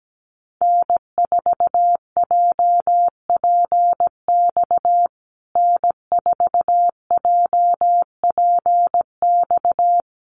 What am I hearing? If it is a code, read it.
Morse code, heard as a keyed tone, about 17 wpm, 700 Hz. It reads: N4JPX N4JPX
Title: USA Call Signs Repeated 3x 17wpm